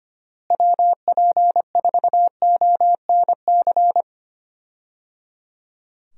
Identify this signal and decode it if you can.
Morse code, heard as a keyed tone, about 25 wpm, 700 Hz. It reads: WP4ONC
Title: USA Call Signs - 6 Characters - Repeated 3x - 25wpm